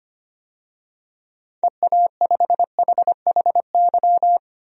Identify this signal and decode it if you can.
Morse code, heard as a keyed tone, about 25 wpm, 700 Hz. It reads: EA5HHY
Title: DXCC-100 Call Signs - 6 Characters - 25wpm